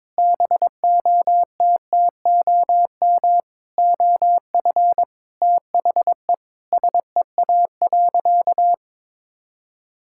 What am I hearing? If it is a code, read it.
Morse code, heard as a keyed tone, about 21 wpm, 700 Hz. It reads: BOTTOM OF THE SEA.
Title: Sentences from Top 1000 Words - Review Part 6 - 22wpm